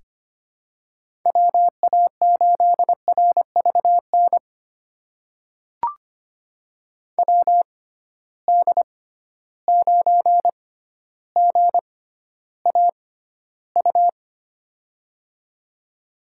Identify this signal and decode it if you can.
Morse code, heard as a keyed tone, about 25 wpm, spaced as for 13 wpm, 700 Hz. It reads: WA8RVN WD9GAU
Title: USA Call Signs - 6x Character Spacing - 25wpm